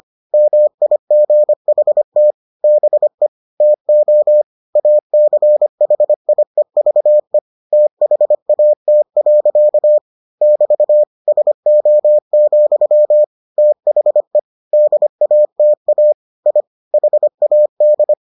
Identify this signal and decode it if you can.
Morse code, heard as a keyed tone, about 25 wpm, 600 Hz. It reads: MIGHT BE TO ACHIEVE THAT. <BT> SO, THE DATA I HAD